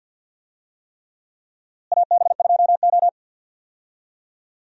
Morse code, heard as a keyed tone, about 49 wpm, 700 Hz. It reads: AB2O
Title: USA Call Signs - 4 Characters - Repeated 3x - 50wpm